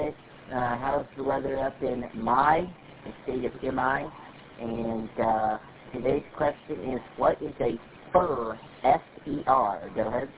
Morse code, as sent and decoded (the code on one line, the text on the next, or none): none